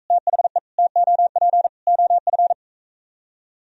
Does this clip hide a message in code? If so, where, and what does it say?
Yes: Morse code from 0.1 to 2.5 s, heard as a keyed tone, about 44 wpm, 700 Hz. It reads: THE TOP OF